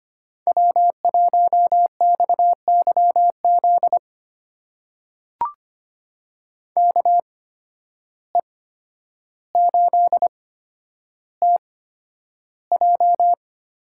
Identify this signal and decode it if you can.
Morse code, heard as a keyed tone, about 25 wpm, spaced as for 11 wpm, 700 Hz. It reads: W1XYZ KE8TJ
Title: USA Call Signs - 5 Characters - 8x Character Spacing - 25wpm